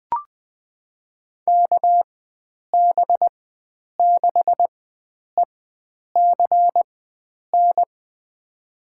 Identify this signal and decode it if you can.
Morse code, heard as a keyed tone, about 20 wpm, spaced as for 9 wpm, 700 Hz. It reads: KB6ECN
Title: USA Call Signs - 6 Characters - 4x Character Spacing - 20wpm